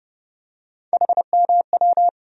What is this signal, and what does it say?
Morse code, heard as a keyed tone, about 30 wpm, 700 Hz. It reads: HMW
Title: USA Call Sign Postfixes - 3 Letters 30wpm